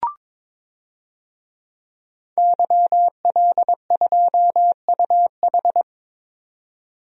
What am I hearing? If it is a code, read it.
Morse code, heard as a keyed tone, about 22 wpm, 700 Hz. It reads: YL2UH
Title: DXCC-100 Call Signs - 5 Characters - 22wpm